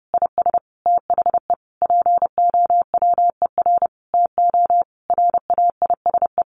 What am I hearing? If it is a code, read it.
Morse code, heard as a keyed tone, about 30 wpm, 700 Hz. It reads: IS THE POWER TO RAISE
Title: Sentences from Top 1400 Words 30wpm